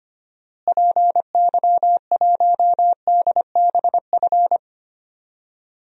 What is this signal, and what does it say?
Morse code, heard as a keyed tone, about 25 wpm, 700 Hz. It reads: PY1DBF
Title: DXCC-100 Call Signs - 6 Characters - Repeated 2x - 25wpm